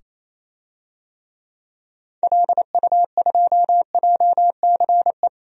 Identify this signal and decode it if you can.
Morse code, heard as a keyed tone, about 27 wpm, 700 Hz. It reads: LU2JCE
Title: DXCC-100 Call Signs - 6 Characters - 28wpm